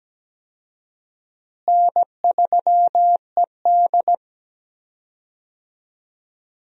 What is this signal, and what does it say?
Morse code, heard as a keyed tone, about 17 wpm, 700 Hz. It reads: N3ED
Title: USA Call Signs - 4 Characters - Repeated 3x - 17wpm